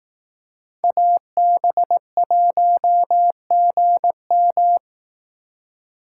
Morse code, heard as a keyed tone, about 18 wpm, 700 Hz. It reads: AB1GM
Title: Speed Racing - USA 2x2 Call Signs 15wpm